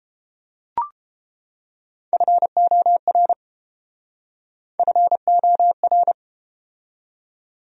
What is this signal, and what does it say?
Morse code, heard as a keyed tone, about 33 wpm, 700 Hz. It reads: FOR FOR